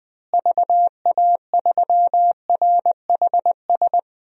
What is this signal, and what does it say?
Morse code, heard as a keyed tone, about 20 wpm, 700 Hz. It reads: VA3RHS
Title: DXCC-100 Call Signs - 6 Characters - Repeated 2x - 20wpm